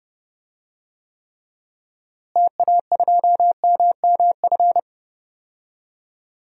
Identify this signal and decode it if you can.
Morse code, heard as a keyed tone, about 30 wpm, 700 Hz. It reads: TA2MMF
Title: DXCC-100 Call Signs - 6 Characters - Repeated 3x - 30wpm